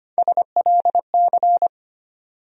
Morse code, heard as a keyed tone, about 25 wpm, 700 Hz. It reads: SLC
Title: USA Call Sign Postfixes - 3 Letters 25wpm